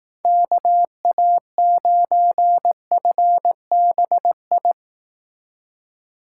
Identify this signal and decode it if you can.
Morse code, heard as a keyed tone, about 18 wpm, 700 Hz. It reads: KA9FBI
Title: Speed Racing - USA 2x3 Call Signs 15wpm